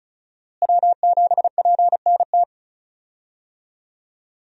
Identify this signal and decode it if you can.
Morse code, heard as a keyed tone, about 34 wpm, 700 Hz. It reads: W7PNT
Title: USA Call Signs - 5 Characters - Repeated 2x - 35wpm